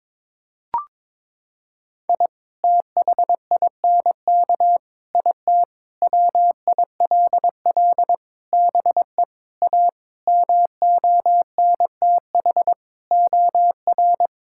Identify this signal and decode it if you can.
Morse code, heard as a keyed tone, about 22 wpm, 700 Hz. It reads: I THINK IT WILL BE A MONTH OR
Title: Sentences from Top 1000 Words - Review Part 5 - 22wpm